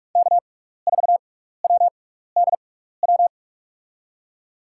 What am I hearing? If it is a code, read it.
Morse code, heard as a keyed tone, about 48 wpm, spaced as for 16 wpm, 700 Hz. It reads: K4WDW
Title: USA Call Signs - 5 Characters - 6x Character Spacing - 45wpm